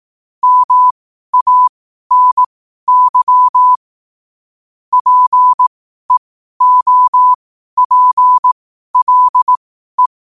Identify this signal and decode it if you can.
Morse code, heard as a keyed tone, about 18 wpm, spaced as for 12 wpm, 1000 Hz. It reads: MANY PEOPLE
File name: qotdfiles / 12wpm